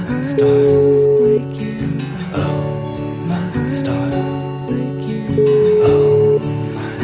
Ringback tone (Europe, 425 Hz), with noise.